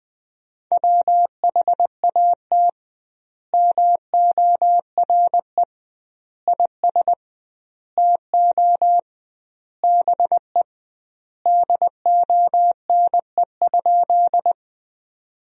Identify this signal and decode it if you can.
Morse code, heard as a keyed tone, about 20 wpm, 700 Hz. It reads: WHAT MORE IS TO BE DONE?
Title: 2x Word-Spacing - Sentences from Top 500 Words - 20wpm